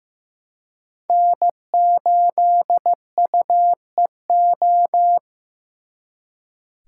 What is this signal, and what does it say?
Morse code, heard as a keyed tone, about 18 wpm, 700 Hz. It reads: N8UEO
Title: Speed Racing - USA 1x3 Call Signs 15wpm